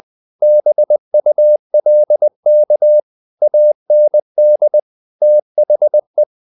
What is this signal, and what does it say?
Morse code, heard as a keyed tone, about 20 wpm, 600 Hz. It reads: BULK AND THE
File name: the old man and the sea part 2 20wpm